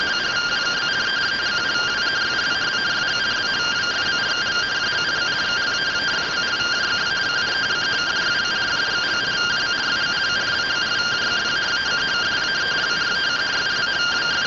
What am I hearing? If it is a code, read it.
Two-tone teleprinter signal, about 170 Hz shift.